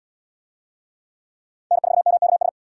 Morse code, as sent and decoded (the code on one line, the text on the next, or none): -. ....- -- -.- ...
N4MKS